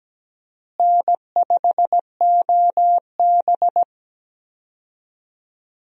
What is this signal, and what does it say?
Morse code, heard as a keyed tone, about 17 wpm, 700 Hz. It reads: N5OB
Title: USA Call Signs - 4 Characters - Repeated 2x - 17wpm